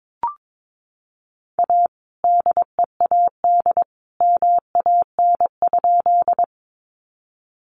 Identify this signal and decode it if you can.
Morse code, heard as a keyed tone, about 22 wpm, 700 Hz. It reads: A DEAD MAN?
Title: Sentences from Top 1500 Words 22wpm